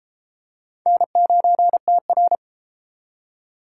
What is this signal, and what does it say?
Morse code, heard as a keyed tone, about 32 wpm, 700 Hz. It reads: N9TR